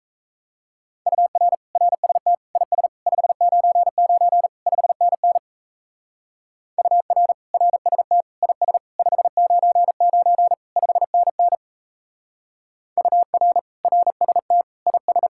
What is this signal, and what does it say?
Morse code, heard as a keyed tone, about 36 wpm, 700 Hz. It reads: UR RST IS 599 5NN UR RST IS 599 5NN UR RST IS